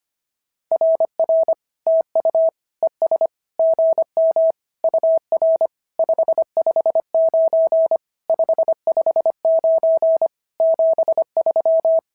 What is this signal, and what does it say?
Morse code, heard as a keyed tone, about 25 wpm, 690 Hz. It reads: RR TU ES GM UR 559 559 73
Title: SOTA QSO Practice 25wpm